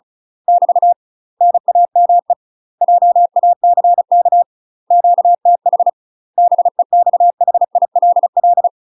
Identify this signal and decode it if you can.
Morse code, heard as a keyed tone, about 36 wpm, 700 Hz. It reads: <BT> NAME JACK QTH BEXHILL
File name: UK MASTER QSOs TRAINING / 35WPM / UK MASTER QSOs 0023